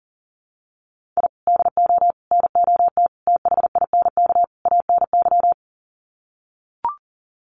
Morse code, heard as a keyed tone, about 40 wpm, 700 Hz. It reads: I DO NOT THINK ANY
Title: Sets of 5 Words Encore 40wpm